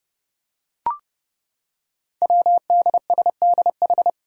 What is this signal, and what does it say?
Morse code, heard as a keyed tone, about 30 wpm, 700 Hz. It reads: WDSDH
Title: Five Challenging Letter-Number Combinations 30wpm